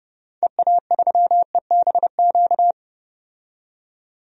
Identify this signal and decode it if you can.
Morse code, heard as a keyed tone, about 30 wpm, 700 Hz. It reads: EA3EBQ
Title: DXCC-100 Call Signs - 6 Characters - 30wpm